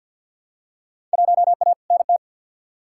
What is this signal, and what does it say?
Morse code, heard as a keyed tone, about 47 wpm, 700 Hz. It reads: 1A NT
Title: Field Day Exchanges Encore - 50wpm